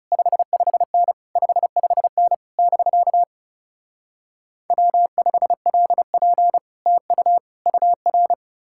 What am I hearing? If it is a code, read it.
Morse code, heard as a keyed tone, about 40 wpm, 700 Hz. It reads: 55N 55N <BK> W5LP TU UR